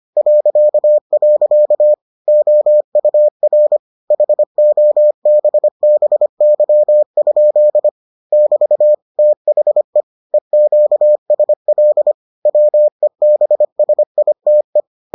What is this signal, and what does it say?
Morse code, heard as a keyed tone, about 25 wpm, 600 Hz. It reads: .. OUR HOBBY? <BT> THE EQSL WEBSITE